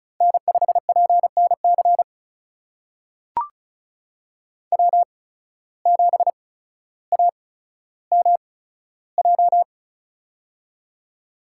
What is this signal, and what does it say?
Morse code, heard as a keyed tone, about 36 wpm, spaced as for 15 wpm, 700 Hz. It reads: N5PNC W7AMJ
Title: USA Call Signs - 5 Characters - 8x Character Spacing - 35wpm